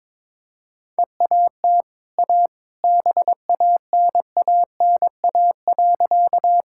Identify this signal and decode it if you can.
Morse code, heard as a keyed tone, about 21 wpm, 690 Hz. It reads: EAT A BANANA.
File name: practice / 22wpm / Easy Sentences with Hints - 22wpm